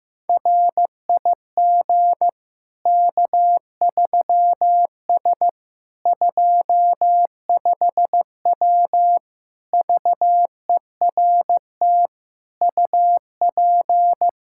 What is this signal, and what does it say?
Morse code, heard as a keyed tone, about 15 wpm, 700 Hz. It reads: RIG K3S 25W VERT UP
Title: Practice QSO - Exchange 2 - Info - 15wpm